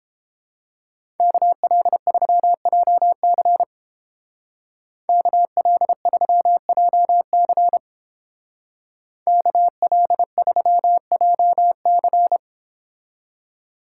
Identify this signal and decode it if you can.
Morse code, heard as a keyed tone, about 31 wpm, 700 Hz. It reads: KL3JC KL3JC KL3JC